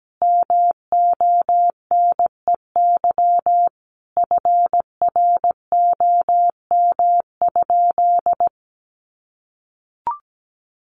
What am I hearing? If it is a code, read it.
Morse code, heard as a keyed tone, about 17 wpm, 700 Hz. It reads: MONEY FROM?
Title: Sentences from Top 400 Words 17wpm